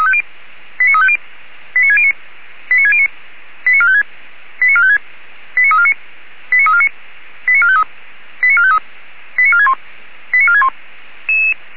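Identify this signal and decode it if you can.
Five-tone selective call (ZVEI), heard as quick runs of five tones, tones of about 70 ms. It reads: 89350 89609 89609 89456 89456 89349 89349 89453 89453 89461 89461
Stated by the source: ZVEI five-tone call 89293 received on SDR, from feuerwehr-orlamuende.org (Orlamünde fire brigade).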